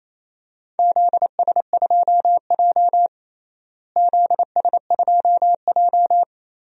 Morse code, heard as a keyed tone, about 28 wpm, 700 Hz. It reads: ZS2J ZS2J
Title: DXCC-100 Call Signs - 4 Characters - Repeated 3x - 28wpm